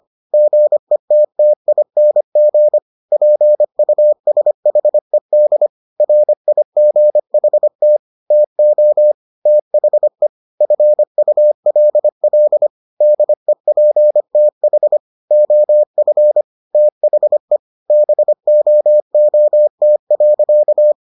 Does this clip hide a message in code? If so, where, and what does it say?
Yes: Morse code from 0.3 to 20.9 s, heard as a keyed tone, about 25 wpm, 600 Hz. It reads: GETTING PUSHED RIGHT TO THE FULL DEPTH OF THE BOOT.